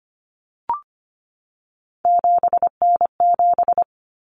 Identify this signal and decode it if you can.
Morse code, heard as a keyed tone, about 25 wpm, 700 Hz. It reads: 7N7
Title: Common Cut Numbers - 3-Digit 25wpm